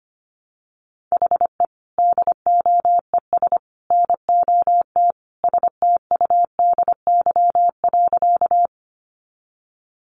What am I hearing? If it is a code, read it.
Morse code, heard as a keyed tone, about 25 wpm, 700 Hz. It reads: HE DOES NOT STUDY.
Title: Additional Easy Phrases 25wpm